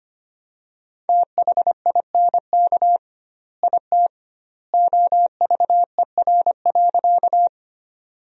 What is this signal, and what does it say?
Morse code, heard as a keyed tone, about 25 wpm, 700 Hz. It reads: THINK IT OVER.
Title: 2x Word-Spacing - Sentences from Top 200 Words - 25wpm